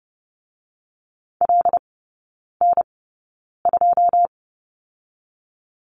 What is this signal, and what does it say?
Morse code, heard as a keyed tone, about 32 wpm, spaced as for 9 wpm, 700 Hz. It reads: LN2
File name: Lesson 16e - Letter c - Three Characters 30wpm